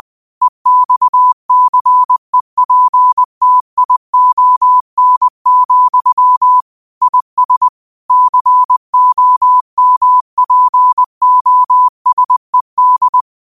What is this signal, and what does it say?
Morse code, heard as a keyed tone, about 20 wpm, 1000 Hz. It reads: EXCEPTION, IS COMPOSED
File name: qotdfiles / 20wpm